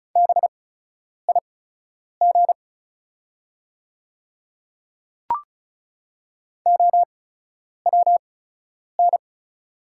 Morse code, heard as a keyed tone, about 36 wpm, spaced as for 10 wpm, 690 Hz. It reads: BIG OWN
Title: 8x Character Spacing - Top 500 Words - 35wpm